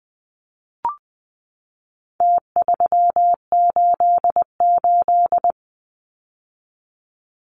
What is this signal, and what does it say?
Morse code, heard as a keyed tone, about 20 wpm, 700 Hz. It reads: T388
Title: Numbers - Contest Style - 4-Digit 20wpm